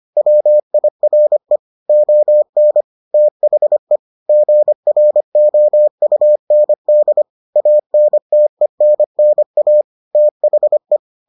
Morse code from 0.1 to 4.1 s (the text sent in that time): WIRE ON THE